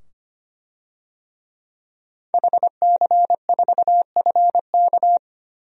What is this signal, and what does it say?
Morse code, heard as a keyed tone, about 25 wpm, 700 Hz. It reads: HC4FK
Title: DXCC-100 Call Signs - 5 Characters - 25wpm